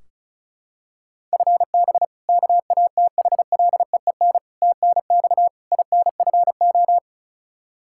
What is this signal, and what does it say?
Morse code, heard as a keyed tone, about 34 wpm, 700 Hz. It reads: FB KATHLEEN TNX INFO
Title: Practice QSO - Exchange 3 - More Info - 35wpm